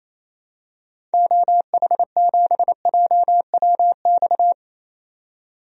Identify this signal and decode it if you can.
Morse code, heard as a keyed tone, about 27 wpm, 700 Hz. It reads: OH7JWX
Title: DXCC-100 Call Signs - 6 Characters - 28wpm